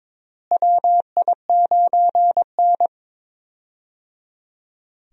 Morse code, heard as a keyed tone, about 22 wpm, 700 Hz. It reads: WI9N
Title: USA Call Signs - 4 Characters - Repeated 3x - 22wpm